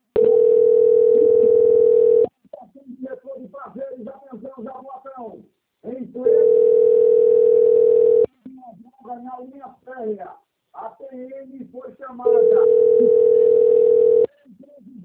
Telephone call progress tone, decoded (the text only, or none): ringback tone